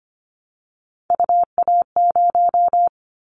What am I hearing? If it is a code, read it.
Morse code, heard as a keyed tone, about 25 wpm, 700 Hz. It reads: UA0
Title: Callsign Prefixes for Top 50 DXCC - 25wpm